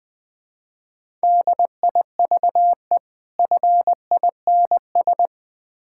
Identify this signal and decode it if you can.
Morse code, heard as a keyed tone, about 21 wpm, 700 Hz. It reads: DIVE FINS
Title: Difficult Words - Speed-Racing with Hints - 20wpm